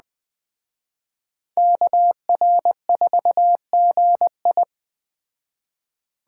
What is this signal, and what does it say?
Morse code, heard as a keyed tone, about 20 wpm, 700 Hz. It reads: KR4GI
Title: Speed Racing - USA 2x2 Call Signs 17wpm